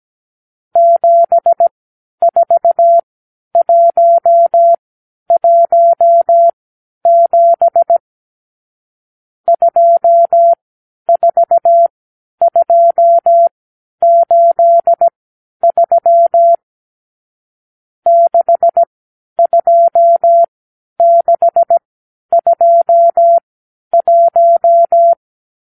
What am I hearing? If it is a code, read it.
Morse code, heard as a keyed tone, about 17 wpm, spaced as for 10 wpm, 680 Hz. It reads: 74117 24283 62621
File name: k7qo / Lesson 54 (9)